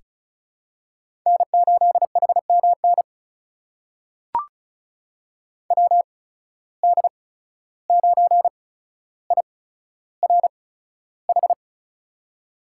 Morse code, heard as a keyed tone, about 36 wpm, spaced as for 14 wpm, 700 Hz. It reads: N8HMN WD9IRH
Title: USA Call Signs - 8x Character Spacing - 35wpm